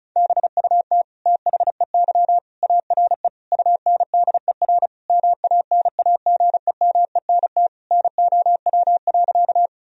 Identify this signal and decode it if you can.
Morse code, heard as a keyed tone, about 36 wpm, 700 Hz. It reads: BUT THEY ARE UNDER MANAGEMENT NOW.
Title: Sentences from Top 1000 Words - Review Part 4 - 35wpm